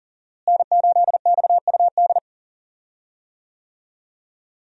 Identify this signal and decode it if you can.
Morse code, heard as a keyed tone, about 40 wpm, 700 Hz. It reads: N8XUD
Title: USA Call Signs - 5 Characters - Repeated 2x - 40wpm